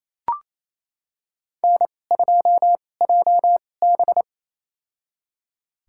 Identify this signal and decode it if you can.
Morse code, heard as a keyed tone, about 28 wpm, spaced as for 20 wpm, 700 Hz. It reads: N2JB
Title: USA Call Signs - 2x Character Spacing - 28wpm